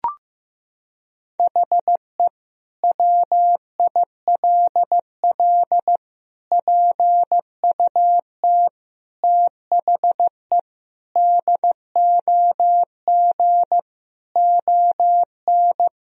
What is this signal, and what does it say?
Morse code, heard as a keyed tone, about 14 wpm, 700 Hz. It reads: HE WILL PUT THE DOG ON
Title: Sentences from Top 1000 Words - Review Part 2 - 15wpm